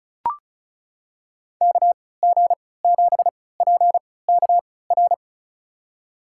Morse code, heard as a keyed tone, about 36 wpm, spaced as for 20 wpm, 700 Hz. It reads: KG7PKR
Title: USA Call Signs - 3x Character Spacing - 35wpm